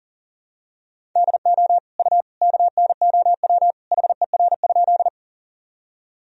Morse code, heard as a keyed tone, about 40 wpm, 700 Hz. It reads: DO U KNOW HER?
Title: Sentences from Top 100 Words and Abbreviations 40wpm